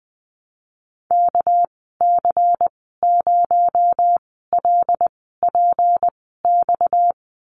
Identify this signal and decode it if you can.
Morse code, heard as a keyed tone, about 19 wpm, spaced as for 14 wpm, 700 Hz. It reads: KC0LPX
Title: 238 Lesson 38g - Letter q - Call Signs 20wpm